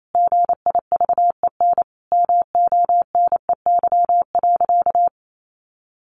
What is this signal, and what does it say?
Morse code, heard as a keyed tone, about 28 wpm, 700 Hz. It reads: GIVEN MONEY.
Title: Sentences from Top 1400 Words 28wpm